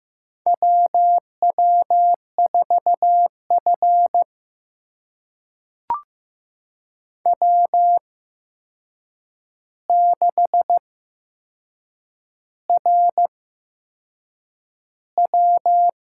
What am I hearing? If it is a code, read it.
Morse code, heard as a keyed tone, about 15 wpm, spaced as for 6 wpm, 700 Hz. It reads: WW4F W6RW
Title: USA Call Signs - 4 Characters - 8x Character Spacing - 15wpm